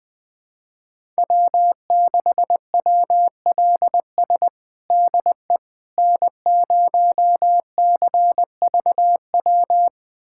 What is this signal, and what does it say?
Morse code, heard as a keyed tone, about 21 wpm, 700 Hz. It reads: W6WLS DE N0CVW
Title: USA Callsign DE Callsign - 5 Characters - 20wpm